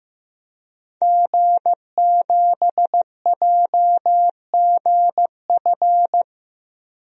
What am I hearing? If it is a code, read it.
Morse code, heard as a keyed tone, about 15 wpm, 700 Hz. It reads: G7JGF
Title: DXCC-100 Call Signs - 5 Characters - 15wpm